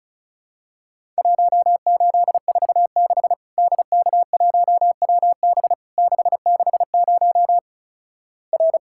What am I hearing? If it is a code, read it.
Morse code, heard as a keyed tone, about 34 wpm, 700 Hz. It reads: 1846 DK1WB 660 R
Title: Worked All Europe DX Contest - QTC Exchange - 35wpm